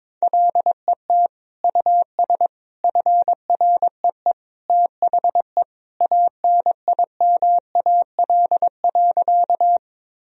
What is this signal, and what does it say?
Morse code, heard as a keyed tone, about 21 wpm, 700 Hz. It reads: LET US FREE THE ANIMAL.
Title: Sentences from Top 1000 Words - Review Part 1 - 22wpm